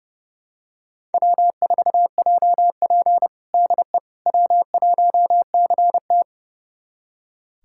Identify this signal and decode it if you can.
Morse code, heard as a keyed tone, about 32 wpm, 700 Hz. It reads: W4JP DE W1CT
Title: USA Callsign DE Callsign - 4 Characters - 30wpm